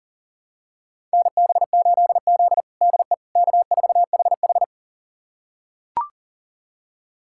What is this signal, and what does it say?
Morse code, heard as a keyed tone, about 40 wpm, 700 Hz. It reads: NB8Z DE K4HH
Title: USA Callsign DE Callsign - 4 Characters - 40wpm